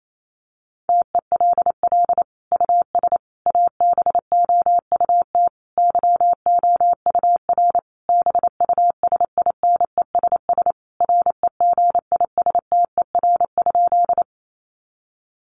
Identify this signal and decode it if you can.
Morse code, heard as a keyed tone, about 28 wpm, 700 Hz. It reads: TELL US ABOUT YOUR BUSINESS REGISTER?
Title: Sentences from Top 1200 Words 28wpm